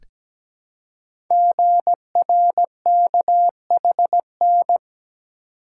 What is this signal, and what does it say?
Morse code, heard as a keyed tone, about 17 wpm, 700 Hz. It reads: GRKHN
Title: Code Groups - 5 Letter:Numbers - 17wpm